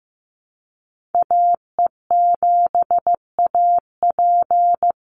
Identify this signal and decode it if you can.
Morse code, heard as a keyed tone, about 15 wpm, 700 Hz. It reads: AE7AP
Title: NA SOTA Chaser Callsigns 15wpm